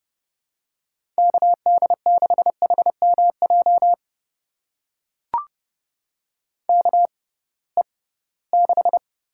Morse code, heard as a keyed tone, about 31 wpm, spaced as for 20 wpm, 700 Hz. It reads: KD6HMJ KE6
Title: USA Call Signs - 6 Characters - 6x Character Spacing - 30wpm